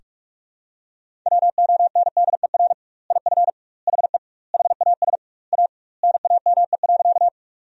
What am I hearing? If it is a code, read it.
Morse code, heard as a keyed tone, about 46 wpm, 700 Hz. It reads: WONDER IF HE HAS A NAME.
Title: 2x Word-Spacing - Sentences from Top 500 Words - 45wpm